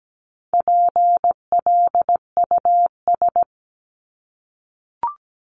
Morse code, heard as a keyed tone, about 17 wpm, 700 Hz. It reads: PLUS
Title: Top 1001-1250 Words 17wpm